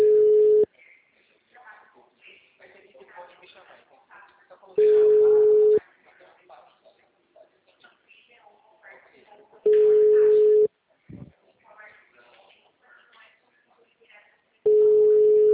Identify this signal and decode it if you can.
Ringback tone (Europe, 425 Hz), with speech.
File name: out- -2007- - - .2